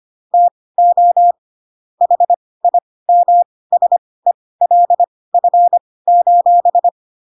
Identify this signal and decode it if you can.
Morse code, heard as a keyed tone, about 25 wpm, spaced as for 18 wpm, 700 Hz. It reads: TO HIMSELF:
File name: Pinocchio / Pinocchio ch5 12 25 ik1hge